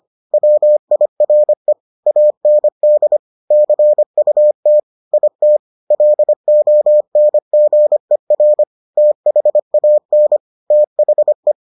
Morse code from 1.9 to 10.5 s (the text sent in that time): AND CUT IT LONGER THAN